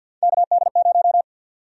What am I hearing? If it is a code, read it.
Morse code, heard as a keyed tone, about 50 wpm, 700 Hz. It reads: KD0
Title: USA Call Sign Prefixes - 50wpm